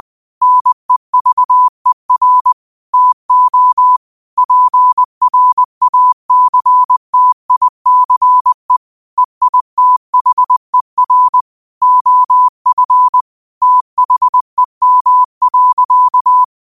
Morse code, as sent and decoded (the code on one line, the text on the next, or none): -. ...- . .-. / - --- / .--. .-. .- -.-. - .. -.-. / . .. - .... .-. / --- ..-. / - .... -- .-.-.-
NEVER TO PRACTICE EITHER OF THEM.